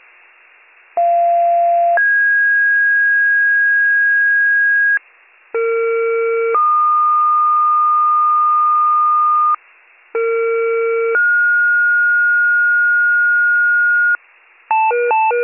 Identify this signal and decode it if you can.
Two-tone page, heard as a one-second tone, then a three-second tone, about 690 then 1650 Hz, about 460 then 1220 Hz, about 460 then 1430 Hz, with noise. It is heard from 1.0 to 14.2 s.